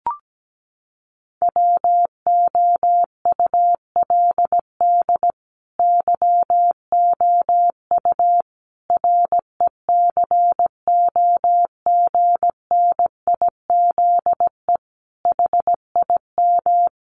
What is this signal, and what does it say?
Morse code, heard as a keyed tone, about 17 wpm, 700 Hz. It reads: WOULD YOU RECOGNIZE HIM
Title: Sentences from Top 1400 Words 17wpm